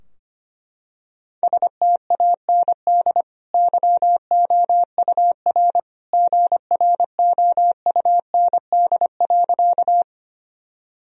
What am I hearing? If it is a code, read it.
Morse code, heard as a keyed tone, about 25 wpm, 700 Hz. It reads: STAND YOUR GROUND.
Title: Sentences from Top 1000 Words - Review Part 2 - 25wpm